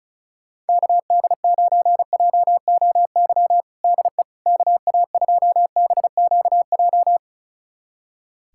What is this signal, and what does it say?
Morse code, heard as a keyed tone, about 36 wpm, 700 Hz. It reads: KD9JOY DE KA2BQJ
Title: USA Callsign DE Callsign - 6 Characters - 35wpm